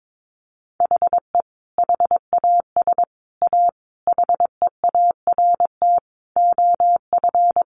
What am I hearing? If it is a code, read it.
Morse code, heard as a keyed tone, about 22 wpm, 700 Hz. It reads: HE HAS A HEART OF
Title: Sentences from Top 1200 Words 22wpm